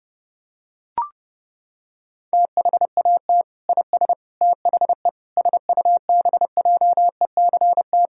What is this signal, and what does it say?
Morse code, heard as a keyed tone, about 30 wpm, 700 Hz. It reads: THAT IS THE SUBJECT
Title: Sentences from Top 1000 Words - Review Part 4 - 30wpm